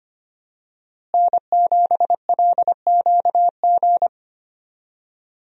Morse code, heard as a keyed tone, about 25 wpm, 700 Hz. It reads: N7LQG
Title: USA Call Signs - 5 Characters - Repeated 3x - 25wpm